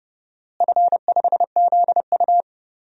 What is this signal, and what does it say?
Morse code, heard as a keyed tone, about 32 wpm, 700 Hz. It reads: F5ZU
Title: DXCC-100 Call Signs - 4 Characters - Repeated 3x - 30wpm